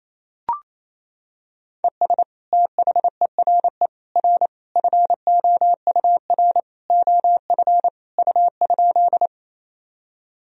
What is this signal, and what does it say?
Morse code, heard as a keyed tone, about 28 wpm, 700 Hz. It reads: ES THERE R FOUR OF U?
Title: Sentences from Top 300 Words and Abbreviations 28wpm